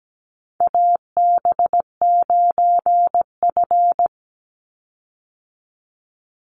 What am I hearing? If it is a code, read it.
Morse code, heard as a keyed tone, about 17 wpm, 700 Hz. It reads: AB9F
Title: USA Call Signs 17wpm